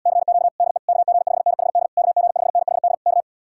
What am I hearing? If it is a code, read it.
Morse code, heard as a keyed tone, about 80 wpm, 700 Hz. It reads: CQ DE KG5NHD KG5NHD K